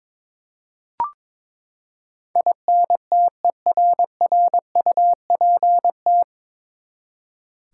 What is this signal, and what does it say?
Morse code, heard as a keyed tone, about 22 wpm, 700 Hz. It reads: INTERRUPT
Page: Morse Code Ninja practice 22wpm